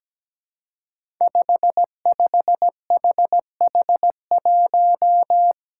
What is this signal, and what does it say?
Morse code, heard as a keyed tone, about 17 wpm, 700 Hz. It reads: 55HH1
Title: Five Challenging Letter-Number Combinations 17wpm